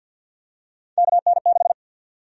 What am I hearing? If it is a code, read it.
Morse code, heard as a keyed tone, about 53 wpm, 700 Hz. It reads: KN6
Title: USA Call Sign Prefixes - 50wpm